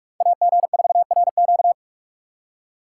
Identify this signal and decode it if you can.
Morse code, heard as a keyed tone, about 44 wpm, 700 Hz. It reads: AG4RQ